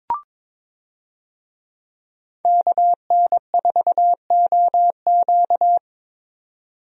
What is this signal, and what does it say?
Morse code, heard as a keyed tone, about 22 wpm, 700 Hz. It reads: KN4OQ
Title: USA Call Signs - 5 Characters - 22wpm